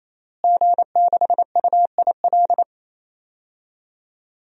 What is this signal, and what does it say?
Morse code, heard as a keyed tone, about 27 wpm, 700 Hz. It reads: G6UIL
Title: DXCC-100 Call Signs - 5 Characters - Repeated 3x - 28wpm